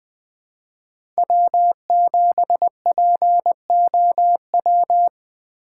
Morse code, heard as a keyed tone, about 21 wpm, 700 Hz. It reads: W7POW
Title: USA Call Signs - 5 Characters - 3x Character Spacing - 20wpm